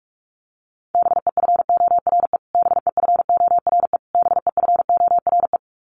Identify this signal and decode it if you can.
Morse code, heard as a keyed tone, about 44 wpm, 700 Hz. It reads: BEFORE BEFORE BEFORE
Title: Top 200 Words Repeated 45wpm